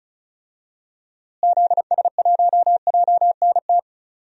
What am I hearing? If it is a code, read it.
Morse code, heard as a keyed tone, about 36 wpm, 700 Hz. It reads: ZS1JNT
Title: DXCC-100 Call Signs - 6 Characters - Repeated 3x - 35wpm